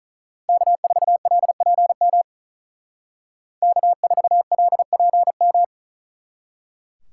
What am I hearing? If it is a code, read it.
Morse code, heard as a keyed tone, about 46 wpm, 700 Hz. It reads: K4LPM K4LPM